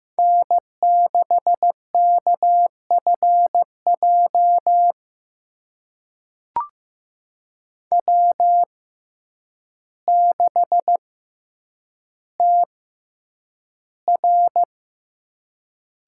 Morse code, heard as a keyed tone, about 15 wpm, spaced as for 8 wpm, 700 Hz. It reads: N6KFJ W6TR
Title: USA Call Signs - 5 Characters - 6x Character Spacing - 15wpm